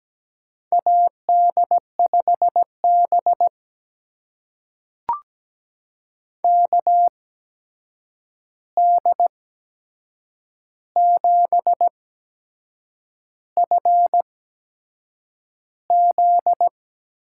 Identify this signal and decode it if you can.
Morse code, heard as a keyed tone, about 17 wpm, spaced as for 7 wpm, 700 Hz. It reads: AD5B KD7FZ
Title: USA Call Signs - 8x Character Spacing - 17wpm